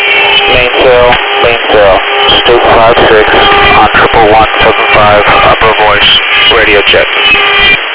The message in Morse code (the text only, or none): none